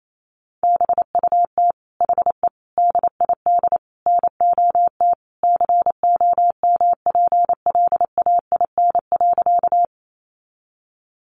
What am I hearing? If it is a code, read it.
Morse code, heard as a keyed tone, about 28 wpm, 700 Hz. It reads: BUT HE DID NOT COMPLAIN.
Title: Sentences from Top 1300 Words 28wpm